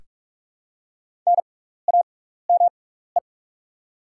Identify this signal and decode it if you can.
Morse code, heard as a keyed tone, about 46 wpm, spaced as for 16 wpm, 690 Hz. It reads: NAME
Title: 6x Character Spacing - Top 200 Words - 45wpm